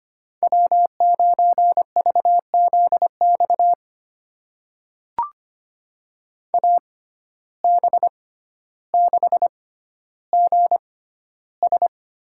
Morse code, heard as a keyed tone, about 25 wpm, spaced as for 13 wpm, 700 Hz. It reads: W9VZX AB6GS
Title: USA Call Signs - 6x Character Spacing - 25wpm